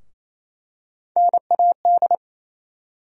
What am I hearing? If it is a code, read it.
Morse code, heard as a keyed tone, about 28 wpm, 700 Hz. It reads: NAD
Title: USA Call Sign Postfixes - 3 Letters 28wpm